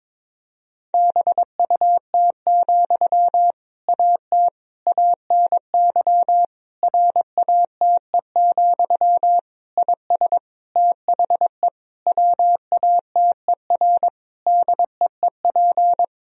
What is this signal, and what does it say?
Morse code, heard as a keyed tone, about 23 wpm, 700 Hz. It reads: BUT, AT ANY RATE, IS THE WATER DEEP
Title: Sentences from Top 1000 Words - Review Part 4 - 22wpm